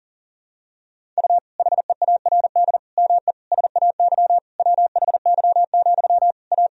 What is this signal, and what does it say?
Morse code, heard as a keyed tone, about 40 wpm, 700 Hz. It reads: U HEARD ME SAY WHY, A
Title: Sentences from Top 500 Words and Abbreviations 40wpm